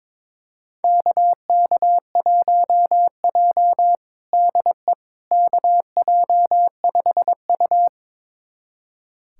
Morse code, heard as a keyed tone, about 22 wpm, 700 Hz. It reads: KK1J DE KJ5U
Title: USA Callsign DE Callsign - 4 Characters - 22wpm